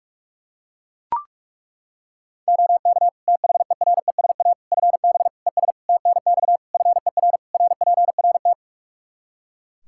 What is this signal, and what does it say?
Morse code, heard as a keyed tone, about 41 wpm, 700 Hz. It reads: OK THERESA FB ES TNX FER RPRT
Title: Practice QSO - Exchange 2 - Info - 45wpm